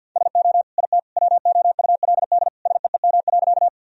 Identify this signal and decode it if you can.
Morse code, heard as a keyed tone, about 51 wpm, 700 Hz. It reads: SO IT WOULD SEEM.
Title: Sentences from Top 1000 Words - Review Part 1 - 50wpm